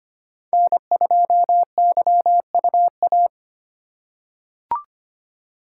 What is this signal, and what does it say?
Morse code, heard as a keyed tone, about 25 wpm, 700 Hz. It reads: N2YUA
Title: USA Call Signs - 5 Characters - 3x Character Spacing - 25wpm